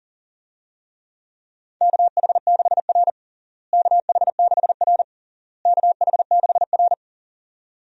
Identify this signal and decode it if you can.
Morse code, heard as a keyed tone, about 37 wpm, 700 Hz. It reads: KH6R KH6R KH6R
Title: USA Call Signs - 4 Characters - Repeated 3x - 40wpm